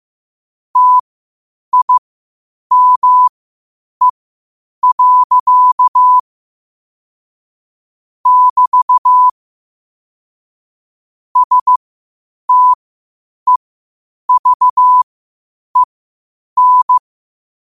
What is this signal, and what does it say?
Morse code, heard as a keyed tone, about 15 wpm, spaced as for 8 wpm, 1000 Hz. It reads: TIME. <BT> STEVEN
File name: qotdfiles / 8wpm